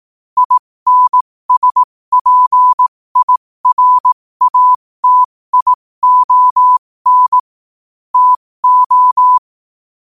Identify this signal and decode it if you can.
Morse code, heard as a keyed tone, about 18 wpm, spaced as for 15 wpm, 1000 Hz. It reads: INSPIRATION TO
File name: qotdfiles / 15wpm